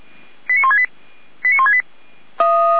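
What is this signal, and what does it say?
Five-tone selective call (ZVEI), heard as quick runs of five tones, tones of about 80 ms. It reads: 89168 89168
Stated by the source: ZVEI five-tone selective call received on an SDR.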